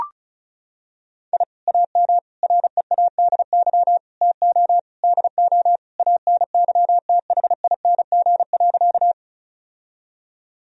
Morse code, heard as a keyed tone, about 34 wpm, 700 Hz. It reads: I AM READY TO DO ANYTHING.
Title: Sentences from Top 1000 Words - Review Part 2 - 35wpm